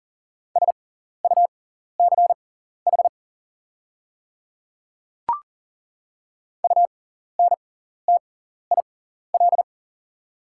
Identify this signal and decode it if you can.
Morse code, heard as a keyed tone, about 38 wpm, spaced as for 14 wpm, 700 Hz. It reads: SUCH UNTIL
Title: 6x Character Spacing - Top 300 Words - 40wpm